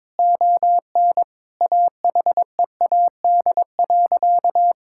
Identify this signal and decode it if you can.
Morse code, heard as a keyed tone, about 22 wpm, 700 Hz. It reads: ON AHEAD.
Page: morse-code-ninja-practice-22wpm